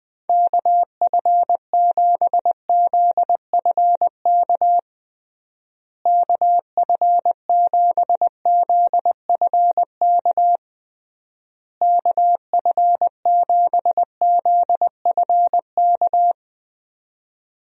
Morse code, heard as a keyed tone, about 20 wpm, 700 Hz. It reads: KF7ZFK KF7ZFK KF7ZFK
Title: USA Call Signs - 6 Characters - Repeated 3x - 20wpm